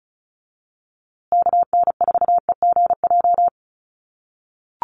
Morse code, heard as a keyed tone, about 34 wpm, 700 Hz. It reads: KN4EGJ
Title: USA Call Signs Repeated 3x 35wpm